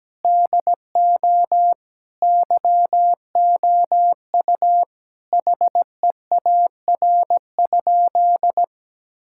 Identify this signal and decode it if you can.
Morse code, heard as a keyed tone, about 17 wpm, 700 Hz. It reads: DO YOU HEAR?
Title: Sentences from Top 1600 Words 17wpm